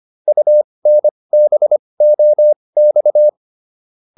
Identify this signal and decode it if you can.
Morse code, heard as a keyed tone, about 25 wpm, spaced as for 20 wpm, 600 Hz. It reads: UNBOX